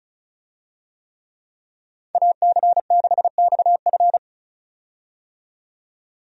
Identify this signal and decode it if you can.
Morse code, heard as a keyed tone, about 36 wpm, 700 Hz. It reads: AC6XF
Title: USA Call Signs - 5 Characters - 35wpm